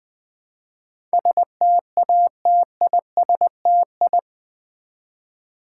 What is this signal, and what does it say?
Morse code, heard as a keyed tone, about 20 wpm, 700 Hz. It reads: STATISTI
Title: Build A Word - 4 Letters At A Time - 11-Letter Words - 20wpm